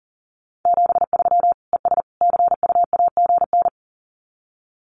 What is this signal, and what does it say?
Morse code, heard as a keyed tone, about 40 wpm, 700 Hz. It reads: 73 ES CU AGN
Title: Common QSO Phrases 40wpm